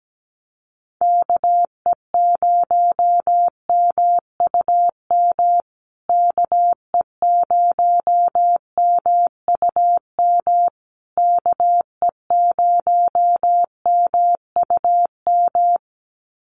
Morse code, heard as a keyed tone, about 17 wpm, 700 Hz. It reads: KE0MUM KE0MUM KE0MUM
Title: USA Call Signs Repeated 3x 17wpm